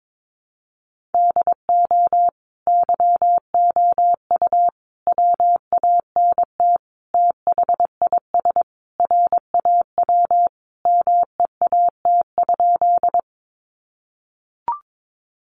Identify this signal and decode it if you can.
Morse code, heard as a keyed tone, about 22 wpm, 700 Hz. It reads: DO YOU WANT THIS RAW MEAT?
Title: Sentences from Top 1500 Words 22wpm